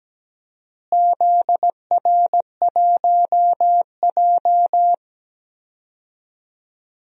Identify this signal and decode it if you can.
Morse code, heard as a keyed tone, about 17 wpm, 700 Hz. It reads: ZR1J
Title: DXCC-100 Call Signs - 4 Characters - Repeated 2x - 17wpm